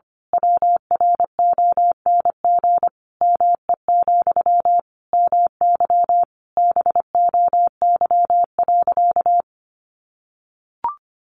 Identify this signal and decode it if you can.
Morse code, heard as a keyed tone, about 25 wpm, 700 Hz. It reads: WRONG ME, MY BOY.
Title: Sentences from Top 900 Words 25wpm